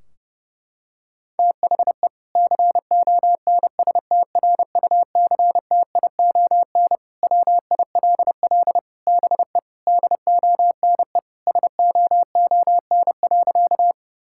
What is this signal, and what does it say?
Morse code, heard as a keyed tone, about 30 wpm, 700 Hz. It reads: THE CONSTRUCTION WILL BE DONE SOON.
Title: Sentences from Top 1600 Words 30wpm